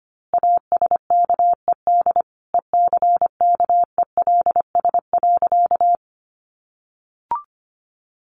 Morse code, heard as a keyed tone, about 25 wpm, 700 Hz. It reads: ASKED ECKELS.
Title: A Sound Of Thunder 25wpm